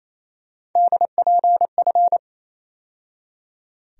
Morse code, heard as a keyed tone, about 28 wpm, 700 Hz. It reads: DPF
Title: USA Call Sign Postfixes - 3 Letters 28wpm